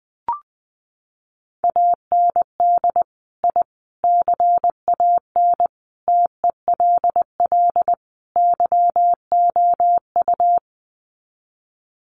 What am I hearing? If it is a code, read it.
Morse code, heard as a keyed tone, about 20 wpm, 700 Hz. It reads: AND I CAN TELL YOU
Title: Sets of 5 Words Encore2 20wpm